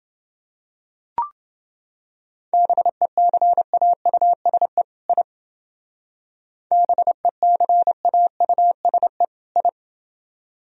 Morse code, heard as a keyed tone, about 28 wpm, 700 Hz. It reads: BECAUSE I BECAUSE I